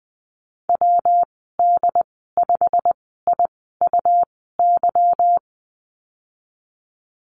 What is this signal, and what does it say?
Morse code, heard as a keyed tone, about 20 wpm, spaced as for 15 wpm, 700 Hz. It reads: WD5IUY
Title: 167 Lesson 27g - Number 7 - Call Signs 20wpm